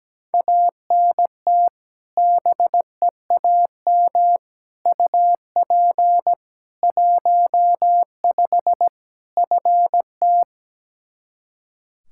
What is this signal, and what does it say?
Morse code, heard as a keyed tone, about 17 wpm, 700 Hz. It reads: ANT BEAM UP 15 FT
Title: Practice QSO - Exchange 2 - Info - 17wpm